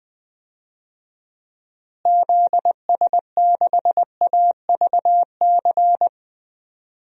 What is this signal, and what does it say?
Morse code, heard as a keyed tone, about 21 wpm, 700 Hz. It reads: ZS6AVC
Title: DXCC-100 Call Signs - 6 Characters - Repeated 3x - 20wpm